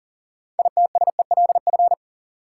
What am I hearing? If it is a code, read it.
Morse code, heard as a keyed tone, about 40 wpm, 700 Hz. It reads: ITSELF
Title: Sentences from Top 800 Words - New Word Drill - 40wpm